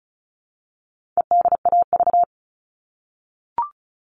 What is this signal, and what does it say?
Morse code, heard as a keyed tone, about 34 wpm, 700 Hz. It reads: EDAV
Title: All Cut Numbers - 4-Digit 35wpm